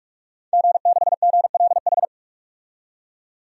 Morse code, heard as a keyed tone, about 47 wpm, 700 Hz. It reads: G6GLH
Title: DXCC-100 Call Signs - 5 Characters - 45wpm